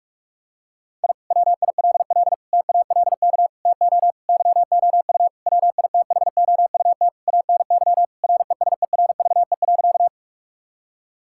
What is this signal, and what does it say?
Morse code, heard as a keyed tone, about 46 wpm, 700 Hz. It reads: I WILL TALK TO YOU WITHOUT ANY RESERVE.
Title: Sentences from Top 1000 Words - Review Part 6 - 45wpm